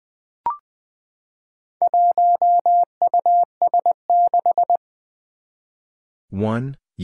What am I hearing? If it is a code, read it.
Morse code, heard as a keyed tone, about 20 wpm, 700 Hz. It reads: 1US6
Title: Four Challenging Letter-Number Combinations 20wpm